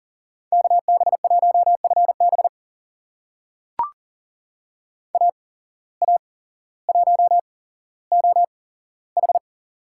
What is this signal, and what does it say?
Morse code, heard as a keyed tone, about 40 wpm, spaced as for 17 wpm, 700 Hz. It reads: KB1FB AA1OH
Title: USA Call Signs - 5 Characters - 8x Character Spacing - 40wpm